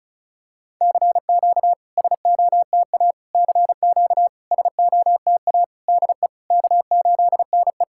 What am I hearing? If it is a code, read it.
Morse code, heard as a keyed tone, about 36 wpm, 690 Hz. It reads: CQ SOTA CQ SOTA DE K8NE
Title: SOTA QSO Practice 35wpm